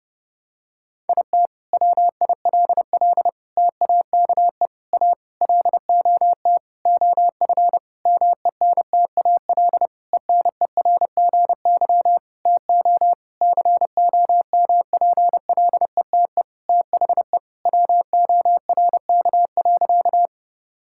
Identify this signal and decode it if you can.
Morse code, heard as a keyed tone, about 28 wpm, 700 Hz. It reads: IT WILL TAKE A LOT OF MENTAL ENERGY TO COMPLETE THE WORK.
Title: Sentences from Top 1000 Words - Review Part 6 - 30wpm